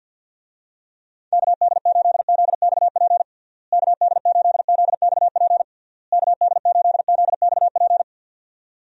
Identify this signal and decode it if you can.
Morse code, heard as a keyed tone, about 49 wpm, 700 Hz. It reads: KD8ZXP KD8ZXP KD8ZXP
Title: USA Call Signs - 6 Characters - Repeated 3x - 50wpm